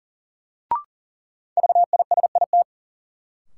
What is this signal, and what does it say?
Morse code, heard as a keyed tone, about 40 wpm, 700 Hz. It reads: VISIT
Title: Sentences from Top 900 Words - New Word Drill - 40wpm